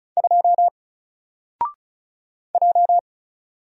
Morse code, heard as a keyed tone, about 33 wpm, 700 Hz. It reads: 2 J
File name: Single-Letter-Number-35wpm